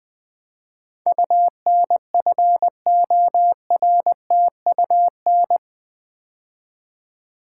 Morse code, heard as a keyed tone, about 20 wpm, 700 Hz. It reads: UNFORTUN
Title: Build A Word - 4 Letters At A Time - 11-Letter Words - 20wpm